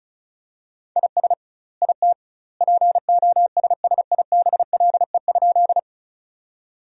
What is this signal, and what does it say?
Morse code, heard as a keyed tone, about 34 wpm, 700 Hz. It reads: IS IT POSSIBLE?
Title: 2x Word-Spacing - Sentences from Top 500 Words - 35wpm